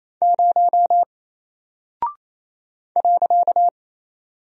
Morse code, heard as a keyed tone, about 28 wpm, 700 Hz. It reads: 0 .
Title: Single Character (Letter:Number:Punctuation:Prosign) Alt 28wpm